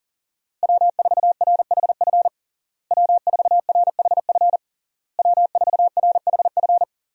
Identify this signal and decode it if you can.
Morse code, heard as a keyed tone, about 40 wpm, 700 Hz. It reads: W4RHF W4RHF W4RHF
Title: USA Call Signs - 5 Characters - Repeated 3x - 40wpm